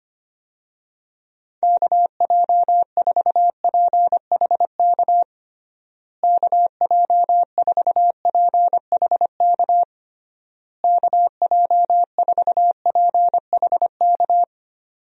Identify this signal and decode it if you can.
Morse code, heard as a keyed tone, about 25 wpm, 700 Hz. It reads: KJ4PHK KJ4PHK KJ4PHK
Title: USA Call Signs - 6 Characters - Repeated 3x - 25wpm